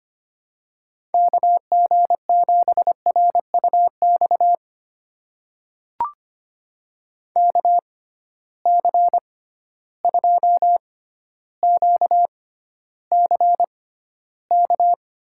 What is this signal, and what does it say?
Morse code, heard as a keyed tone, about 25 wpm, spaced as for 13 wpm, 700 Hz. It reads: KG7RUX KC2QCK
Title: USA Call Signs - 6 Characters - 6x Character Spacing - 25wpm